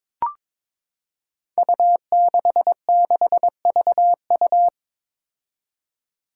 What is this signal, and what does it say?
Morse code, heard as a keyed tone, about 22 wpm, 700 Hz. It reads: U66VU
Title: Five Challenging Letter-Number Combinations 22wpm